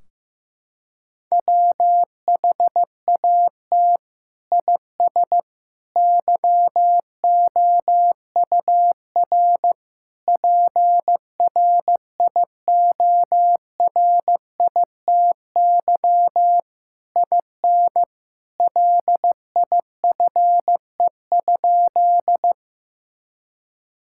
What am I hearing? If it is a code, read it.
Morse code, heard as a keyed tone, about 15 wpm, 700 Hz. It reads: WHAT IS YOUR PRIORITY IN LIFE?
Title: Sentences from Top 1600 Words 15wpm